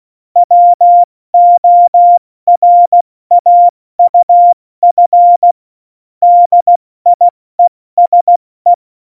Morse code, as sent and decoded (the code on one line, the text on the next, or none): .-- --- .-. .- ..- ..-. / -.. .. ... .
WORAUF DIESE